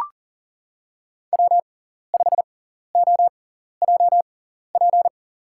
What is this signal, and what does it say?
Morse code, heard as a keyed tone, about 40 wpm, spaced as for 14 wpm, 700 Hz. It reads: W5OJP
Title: USA Call Signs - 5 Characters - 6x Character Spacing - 40wpm